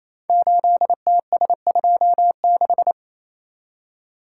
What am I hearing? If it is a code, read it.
Morse code, heard as a keyed tone, about 27 wpm, 700 Hz. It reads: 8TS26
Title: Code Groups - 5 Letter:Numbers - 28wpm